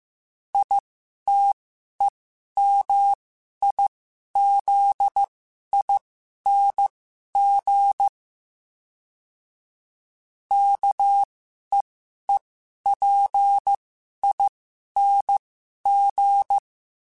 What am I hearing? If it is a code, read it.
Morse code, heard as a keyed tone, about 15 wpm, spaced as for 11 wpm, 800 Hz. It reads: ITEMIZING KEEPING
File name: Special INGs-mp3s / Special INGs-mp3s / Sngl Ing @10